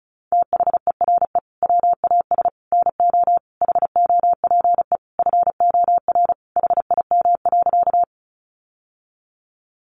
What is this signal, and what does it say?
Morse code, heard as a keyed tone, about 34 wpm, 700 Hz. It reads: THERE WAS NO HOPE FOR HIM.
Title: Sentences from Top 1000 Words 35wpm